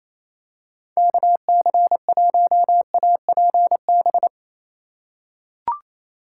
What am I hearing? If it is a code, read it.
Morse code, heard as a keyed tone, about 28 wpm, 700 Hz. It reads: KC1APB
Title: USA Call Signs - 6 Characters - 3x Character Spacing - 28wpm